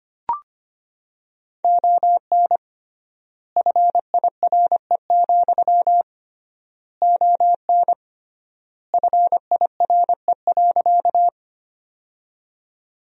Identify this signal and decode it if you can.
Morse code, heard as a keyed tone, about 25 wpm, 700 Hz. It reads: ON FIRE, ON FIRE.
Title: 3x Word-Spacing - Sentences from Top 400 Words - 25wpm